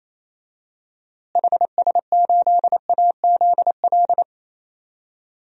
Morse code, heard as a keyed tone, about 28 wpm, 700 Hz. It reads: HS8AZL
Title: DXCC-100 Call Signs - 6 Characters - Repeated 2x - 28wpm